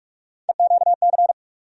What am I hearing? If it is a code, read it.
Morse code, heard as a keyed tone, about 46 wpm, 700 Hz. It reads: EQC
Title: USA Call Sign Postfixes - 3 Letters 45wpm